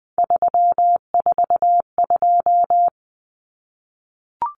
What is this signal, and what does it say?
Morse code, heard as a keyed tone, about 20 wpm, 700 Hz. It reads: 342
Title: Numbers - 3-Digit 20wpm